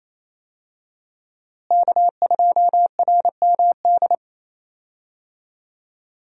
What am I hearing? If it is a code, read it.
Morse code, heard as a keyed tone, about 28 wpm, 700 Hz. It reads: K2RMD
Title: USA Call Signs - 5 Characters - Repeated 2x - 28wpm